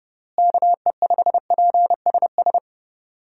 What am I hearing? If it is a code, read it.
Morse code, heard as a keyed tone, about 31 wpm, 700 Hz. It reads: KE5PSS
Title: USA Call Signs - 6 Characters - Repeated 3x - 30wpm